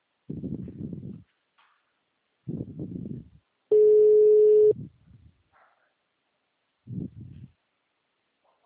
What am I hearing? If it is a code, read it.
Ringback tone (Europe, 425 Hz).
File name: out- -2049- - - .43